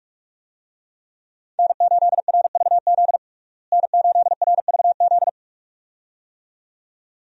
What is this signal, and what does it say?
Morse code, heard as a keyed tone, about 42 wpm, 700 Hz. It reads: N8RVZ N8RVZ
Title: USA Call Signs - 5 Characters - Repeated 2x - 45wpm